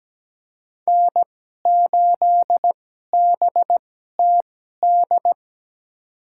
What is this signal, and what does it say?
Morse code, heard as a keyed tone, about 17 wpm, spaced as for 12 wpm, 700 Hz. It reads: N8BTD
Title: USA Call Signs - 2x Character Spacing - 17wpm